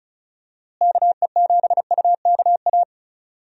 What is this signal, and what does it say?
Morse code, heard as a keyed tone, about 34 wpm, 700 Hz. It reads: KE7UKA
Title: USA Call Signs - 6 Characters - Repeated 2x - 35wpm